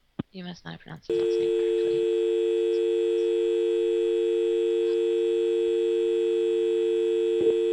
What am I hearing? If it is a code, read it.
Dial tone (UK), 350 + 450 Hz, with speech.